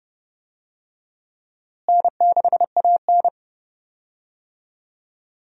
Morse code, heard as a keyed tone, about 30 wpm, 700 Hz. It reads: N6AN
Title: USA Call Signs - 4 Characters - Repeated 3x - 30wpm